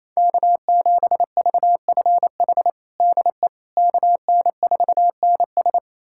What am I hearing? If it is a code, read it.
Morse code, heard as a keyed tone, about 28 wpm, 700 Hz. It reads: K7VFH DE KN4NS
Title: USA Callsign DE Callsign - 5 Characters - 28wpm